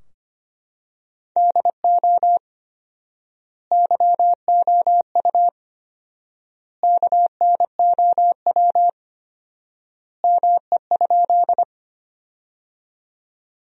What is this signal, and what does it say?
Morse code, heard as a keyed tone, about 25 wpm, 700 Hz. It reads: DO YOU KNOW ME?
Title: 4x Word-Spacing - Sentences from Top 200 Words - 25wpm